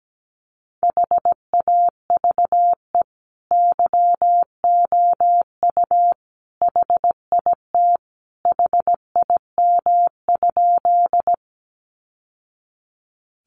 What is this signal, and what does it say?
Morse code, heard as a keyed tone, about 17 wpm, 700 Hz. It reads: HAVE YOU HIT HIM?
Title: Sentences from Top 1200 Words 17wpm